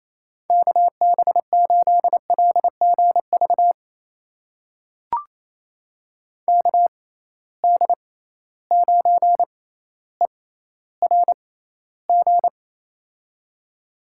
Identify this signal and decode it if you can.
Morse code, heard as a keyed tone, about 28 wpm, spaced as for 14 wpm, 700 Hz. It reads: KB8LGV KD9ERG
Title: USA Call Signs - 6 Characters - 6x Character Spacing - 28wpm